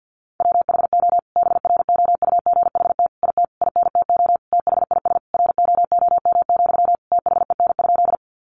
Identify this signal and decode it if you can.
Morse code, heard as a keyed tone, about 48 wpm, 700 Hz. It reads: WHO BROUGHT IT INTO THIS ROOM, THEN?
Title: Sentences from Top 600 Words 50wpm